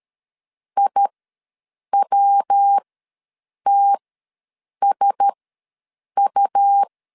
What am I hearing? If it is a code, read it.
Morse code, heard as a keyed tone, about 13 wpm, spaced as for 7 wpm, 800 Hz. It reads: IWTSU
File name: LEZIONI / 5 LBVC / 5 L ws 12wpm 2958